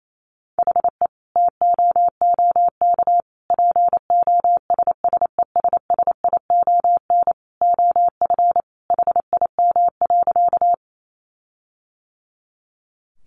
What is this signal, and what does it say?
Morse code, heard as a keyed tone, about 28 wpm, 700 Hz. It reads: HE TOOK POSSESSION OF HIM.
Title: Sentences from Top 1400 Words 28wpm